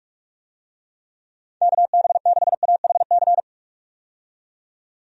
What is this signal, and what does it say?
Morse code, heard as a keyed tone, about 41 wpm, 700 Hz. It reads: KB6AHC
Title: USA Call Signs - 6 Characters - 45wpm